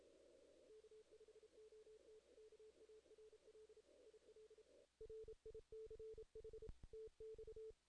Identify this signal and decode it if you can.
Morse code, heard as a keyed tone, about 31 wpm, 440 Hz. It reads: K5OT KARL IL RICH TX
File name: 2016 JAN CW NAQP / 2016 Jan naqp k5ot - 0-11